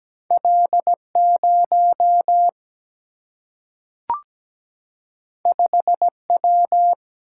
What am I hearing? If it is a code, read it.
Morse code, heard as a keyed tone, about 17 wpm, 700 Hz. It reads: L0 5W
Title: Single-Letter Single-Number 17wpm